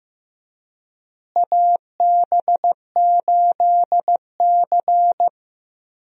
Morse code, heard as a keyed tone, about 15 wpm, 700 Hz. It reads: AB8C
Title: USA Call Signs - 4 Characters - Repeated 2x - 15wpm